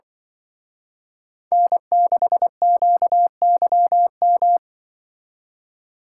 Morse code, heard as a keyed tone, about 24 wpm, 700 Hz. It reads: N6QYM
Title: Speed Racing - USA Call Signs 20wpm